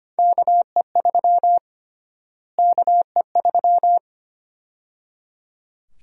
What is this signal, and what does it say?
Morse code, heard as a keyed tone, about 25 wpm, 700 Hz. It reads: KE3 KE3
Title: USA Call Sign Prefixes - Repeated 2x - 25wpm